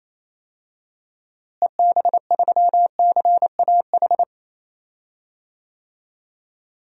Morse code, heard as a keyed tone, about 27 wpm, 700 Hz. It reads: EB3CAH
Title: DXCC-100 Call Signs - 6 Characters - Repeated 2x - 28wpm